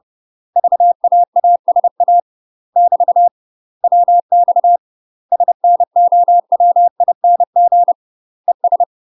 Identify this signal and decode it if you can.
Morse code, heard as a keyed tone, about 30 wpm, 700 Hz. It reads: VAASA <BT> WX SNOWING ES
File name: UK MASTER QSOs TRAINING / 30WPM / UK MASTER QSOs 0031